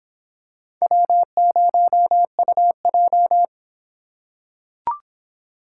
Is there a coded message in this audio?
Yes: Morse code from 0.8 to 3.4 s, heard as a keyed tone, about 26 wpm, 700 Hz. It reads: W0UJ